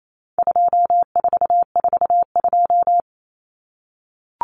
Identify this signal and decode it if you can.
Morse code, heard as a keyed tone, about 28 wpm, 700 Hz. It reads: 2442
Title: Numbers - 4-Digit 28wpm